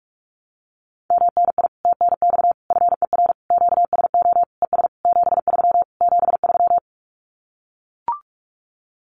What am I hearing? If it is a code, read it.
Morse code, heard as a keyed tone, about 44 wpm, 700 Hz. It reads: MNI TNX FER QSO ES 73 73
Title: Common QSO Phrases Encore 45wpm